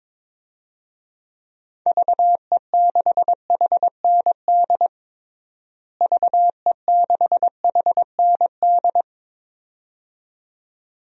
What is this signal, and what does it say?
Morse code, heard as a keyed tone, about 22 wpm, 700 Hz. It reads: VE6HND VE6HND
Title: DXCC-100 Call Signs - 6 Characters - Repeated 2x - 22wpm